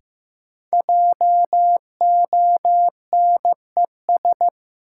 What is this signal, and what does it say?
Morse code, heard as a keyed tone, about 16 wpm, 700 Hz. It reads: JONES
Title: Sets of 5 Words - New Word Drill - 15wpm